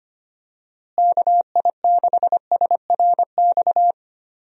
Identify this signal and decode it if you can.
Morse code, heard as a keyed tone, about 25 wpm, 700 Hz. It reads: KI6SRX
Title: USA Call Signs - 6 Characters - Repeated 2x - 25wpm